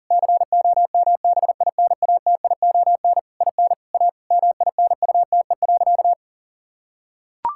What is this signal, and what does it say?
Morse code, heard as a keyed tone, about 40 wpm, 700 Hz. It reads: COMBINATION IN A MINUTE.
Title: Sentences from Top 1000 Words - Review Part 6 - 40wpm